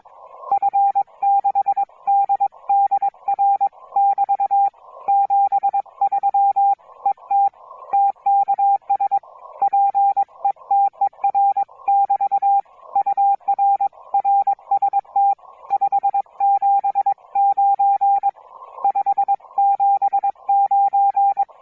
Morse code, heard as a keyed tone, about 22 wpm, 810 Hz. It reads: F6DDR <BT> 73 ET TKS PETER<BT> UR RST 579 579